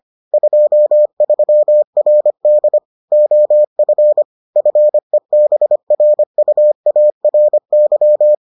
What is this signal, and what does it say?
Morse code, heard as a keyed tone, about 25 wpm, 600 Hz. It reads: 23RD OF FEBRUARY